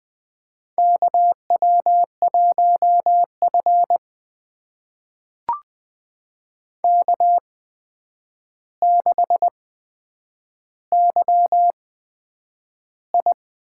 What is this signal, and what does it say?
Morse code, heard as a keyed tone, about 20 wpm, spaced as for 9 wpm, 700 Hz. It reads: KW1F K6YI
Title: USA Call Signs - 4 Characters - 8x Character Spacing - 20wpm